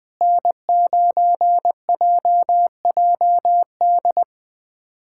Morse code, heard as a keyed tone, about 20 wpm, 700 Hz. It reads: N9JJD
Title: USA Callsign DE Callsign 20wpm